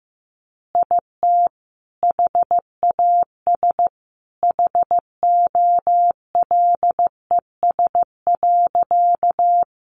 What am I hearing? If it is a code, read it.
Morse code, heard as a keyed tone, about 15 wpm, 700 Hz. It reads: IT HAS HOLES.
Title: Additional Easy Phrases 15wpm